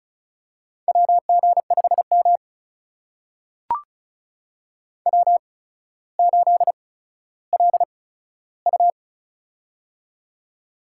Morse code, heard as a keyed tone, about 36 wpm, spaced as for 15 wpm, 700 Hz. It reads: WG5M W8LU
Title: USA Call Signs - 4 Characters - 8x Character Spacing - 35wpm